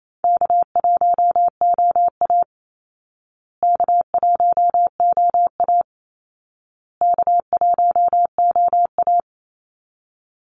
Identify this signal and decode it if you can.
Morse code, heard as a keyed tone, about 28 wpm, 700 Hz. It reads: K1OA K1OA K1OA
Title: Call Signs Repeated 2x - 28wpm